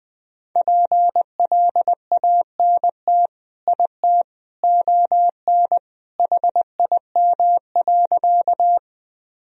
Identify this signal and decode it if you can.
Morse code, heard as a keyed tone, about 21 wpm, 700 Hz. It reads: PLANT IT ON HIM.
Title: Sentences from Top 1000 Words - Review Part 1 - 20wpm